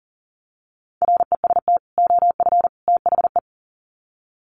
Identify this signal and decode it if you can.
Morse code, heard as a keyed tone, about 40 wpm, 700 Hz. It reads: REST OF THE
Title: Sets of 3 Words 40wpm